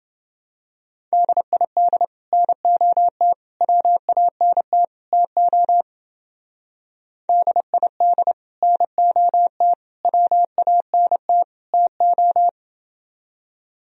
Morse code, heard as a keyed tone, about 28 wpm, 700 Hz. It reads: DID NOT WANT TO DID NOT WANT TO